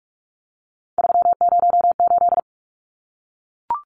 Morse code, heard as a keyed tone, about 44 wpm, 700 Hz. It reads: 308
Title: Numbers - 3-Digit 45wpm